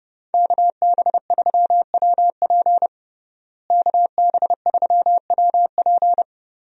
Morse code, heard as a keyed tone, about 30 wpm, 700 Hz. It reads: KB3WP KB3WP
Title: USA Call Signs - 5 Characters - Repeated 2x - 30wpm